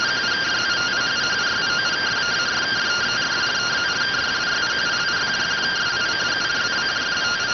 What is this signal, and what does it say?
Two-tone teleprinter signal, about 170 Hz shift.